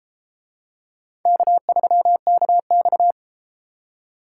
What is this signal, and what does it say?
Morse code, heard as a keyed tone, about 32 wpm, 700 Hz. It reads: K3KX